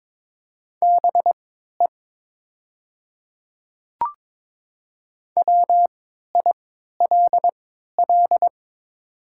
Morse code, heard as a keyed tone, about 22 wpm, spaced as for 12 wpm, 700 Hz. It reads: BE WILL
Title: 3x Character Spacing - Top 100 Words - 22wpm